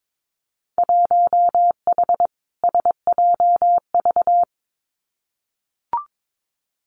Morse code, heard as a keyed tone, about 22 wpm, 700 Hz. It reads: 1H SJV
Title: Winter Field Day Exchanges 22wpm